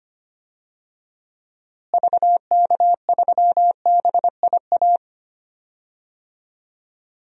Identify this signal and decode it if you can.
Morse code, heard as a keyed tone, about 25 wpm, 700 Hz. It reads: VK3BIA
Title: DXCC-100 Call Signs - 6 Characters - Repeated 2x - 25wpm